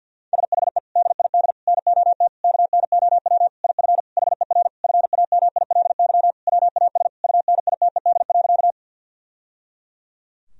Morse code, heard as a keyed tone, about 50 wpm, 700 Hz. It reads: SHE DID NOT KNOW IF HER FAMILY WAS UNITED.
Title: Sentences from Top 1000 Words - Review Part 4 - 50wpm